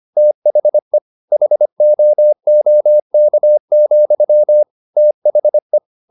Morse code, heard as a keyed tone, about 25 wpm, 600 Hz. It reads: THE HOOK, THE
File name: the old man and the sea part 1 25wpm